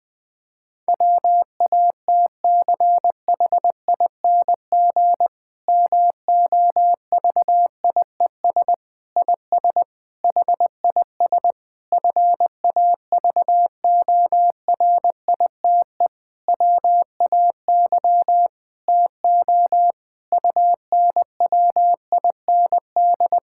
Morse code, heard as a keyed tone, about 20 wpm, 700 Hz. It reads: WATCHING MOVIES IS HIS FAVORITE WAY TO UNWIND
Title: Difficult Sentences with Hints - 20wpm